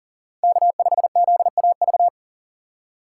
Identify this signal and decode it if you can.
Morse code, heard as a keyed tone, about 40 wpm, 700 Hz. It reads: K5ZAV
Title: USA Call Signs - 5 Characters - 2x Character Spacing - 40wpm